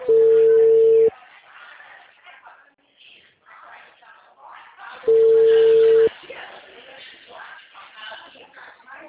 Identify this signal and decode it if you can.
Ringback tone (China, 450 Hz), with speech.